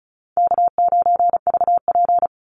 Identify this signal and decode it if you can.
Morse code, heard as a keyed tone, about 34 wpm, 700 Hz. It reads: K9VP
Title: Call Signs - 35wpm